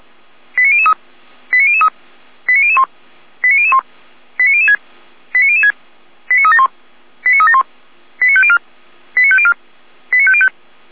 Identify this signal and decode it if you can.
Five-tone selective call (ZVEI), heard as quick runs of five tones, tones of about 70 ms. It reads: 89003 89003 89001 89001 89006 89006 89371 89371 89504 89504 89505